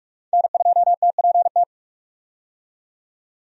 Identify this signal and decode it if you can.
Morse code, heard as a keyed tone, about 46 wpm, 700 Hz. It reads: N2TPT
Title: USA Call Signs - 5 Characters - 45wpm